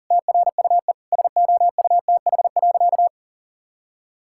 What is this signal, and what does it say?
Morse code, heard as a keyed tone, about 37 wpm, 700 Hz. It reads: TRUE SOUTH.
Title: Sentences from Top 1000 Words - Review Part 6 - 40wpm